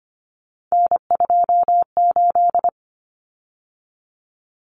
Morse code, heard as a keyed tone, about 25 wpm, 700 Hz. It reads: N28
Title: Common Cut Numbers - 3-Digit 25wpm